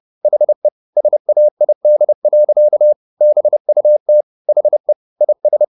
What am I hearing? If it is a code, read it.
Morse code, heard as a keyed tone, about 30 wpm, 600 Hz. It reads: HE SAID. BUT HE IS